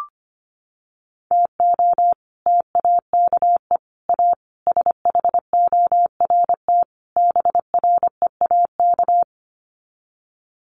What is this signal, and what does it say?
Morse code, heard as a keyed tone, about 25 wpm, 700 Hz. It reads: TO TAKE A SHORT BREAK
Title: Sets of 5 Words Encore2 25wpm